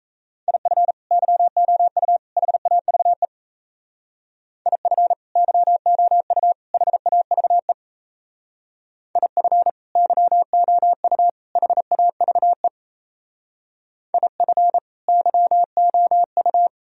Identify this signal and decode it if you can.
Morse code, heard as a keyed tone, about 34 wpm, 700 Hz. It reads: IF YOU HAVE IF YOU HAVE IF YOU HAVE IF YOU
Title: Speed-Racing - Sets of 3 Words 28wpm